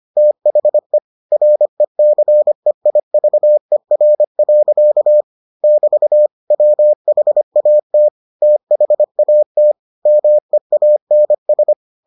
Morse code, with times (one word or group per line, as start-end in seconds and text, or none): 0.2-1.0 THE
1.3-5.2 RECEIVER.
5.6-6.3 <BT>
6.5-8.1 WHAT
8.4-9.7 THAT
10.1-11.7 MEANS